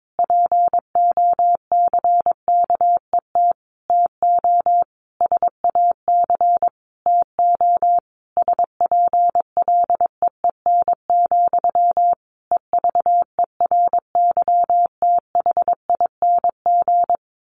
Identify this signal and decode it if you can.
Morse code, heard as a keyed tone, about 22 wpm, 700 Hz. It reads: POCKET TO SAC TO SPLEEN, EVERYTHING
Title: A Sound Of Thunder 22wpm